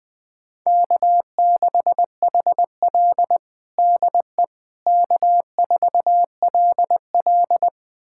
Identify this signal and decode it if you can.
Morse code, heard as a keyed tone, about 21 wpm, 700 Hz. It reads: K6HL DE K4LL
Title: USA Callsign DE Callsign - 4 Characters - 20wpm